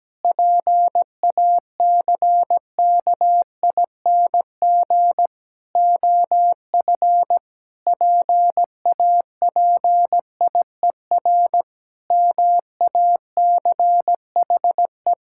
Morse code, heard as a keyed tone, about 17 wpm, 700 Hz. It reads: PACKING OF PAPIER MACHE
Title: Pangram Challenge v2 - 17wpm